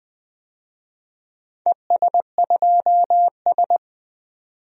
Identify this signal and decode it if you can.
Morse code, heard as a keyed tone, about 20 wpm, 700 Hz. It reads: ES2S
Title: DXCC-100 Call Signs - 4 Characters - 20wpm